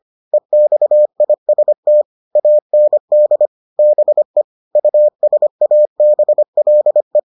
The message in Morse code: EXIST AND BE USABLE

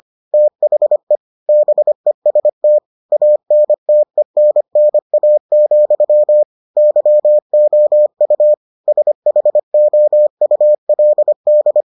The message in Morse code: THE BEST ANTENNA, YOU SHOULD

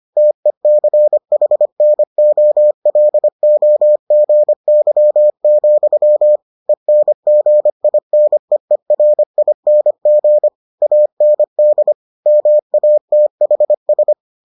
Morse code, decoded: TECHNOLOGY, ENGINEERING AND MATHS

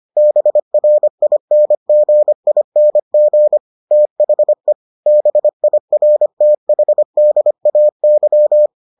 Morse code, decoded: BRINGING THE BIRTHDAY

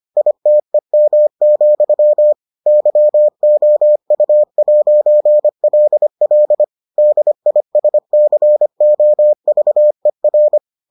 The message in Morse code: ITEM, YOU'LL DISCOVER

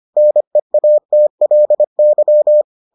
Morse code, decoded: NEATLY

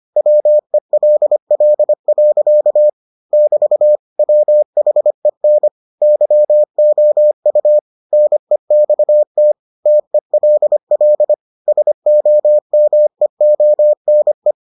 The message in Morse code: WELL. <BT> WHEN YOU NEXT TELL SOMEONE

